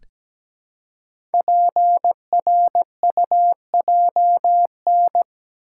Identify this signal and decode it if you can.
Morse code, heard as a keyed tone, about 17 wpm, 700 Hz. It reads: PRUJN
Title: Code Groups - 5 Letters - 17wpm